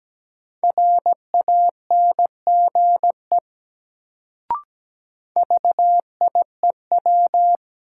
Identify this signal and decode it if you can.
Morse code, heard as a keyed tone, about 17 wpm, 700 Hz. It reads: RANGE VIEW
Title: Sentences from Top 900 Words - New Word Drill - 17wpm